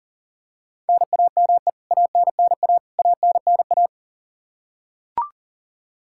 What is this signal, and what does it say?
Morse code, heard as a keyed tone, about 40 wpm, 700 Hz. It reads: NAME ANNA ANNA
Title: K1USN SST QSO Practice 40wpm